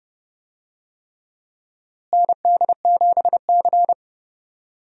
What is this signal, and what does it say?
Morse code, heard as a keyed tone, about 30 wpm, 700 Hz. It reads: ND7C
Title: USA Call Signs - 4 Characters - Repeated 3x - 30wpm